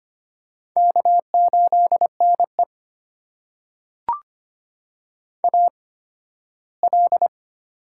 Morse code, heard as a keyed tone, about 25 wpm, spaced as for 15 wpm, 700 Hz. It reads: K8NE AL
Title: USA Call Signs - 8x Character Spacing - 25wpm